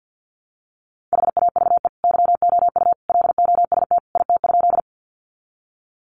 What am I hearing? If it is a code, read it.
Morse code, heard as a keyed tone, about 50 wpm, 700 Hz. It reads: HAVE YOU LOST IT?
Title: Sentences from Top 1400 Words 50wpm